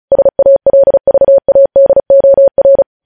Morse code, heard as a keyed tone, about 36 wpm, 550 Hz. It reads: SALVADOR
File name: podcast / News headlines-1x-35x35